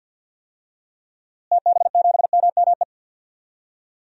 Morse code, heard as a keyed tone, about 48 wpm, 700 Hz. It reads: TB7MME
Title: DXCC-100 Call Signs - 6 Characters - Repeated 3x - 50wpm